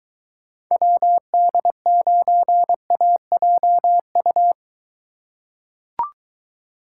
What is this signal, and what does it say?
Morse code, heard as a keyed tone, about 23 wpm, 700 Hz. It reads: WD9AJU